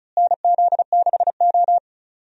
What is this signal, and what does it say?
Morse code, heard as a keyed tone, about 34 wpm, 700 Hz. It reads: NZ6O
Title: USA Call Signs - 4 Characters - Repeated 3x - 35wpm